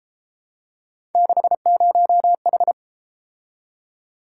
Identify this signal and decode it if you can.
Morse code, heard as a keyed tone, about 31 wpm, 700 Hz. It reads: 60H